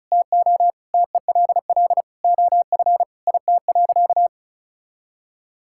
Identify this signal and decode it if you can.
Morse code, heard as a keyed tone, about 36 wpm, 700 Hz. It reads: TO TELL OF IT.
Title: Sentences from Top 1600 Words 35wpm